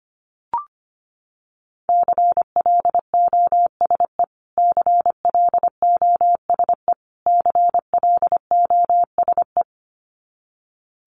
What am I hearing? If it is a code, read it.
Morse code, heard as a keyed tone, about 25 wpm, 700 Hz. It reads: CLOSE CLOSE CLOSE
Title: Top 300 Words Repeated 25wpm